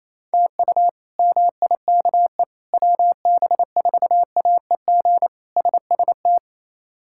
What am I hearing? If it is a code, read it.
Morse code, heard as a keyed tone, about 28 wpm, 700 Hz. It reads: TU MIKE WB4AEG SST
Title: K1USN SST QSO Practice - 28wpm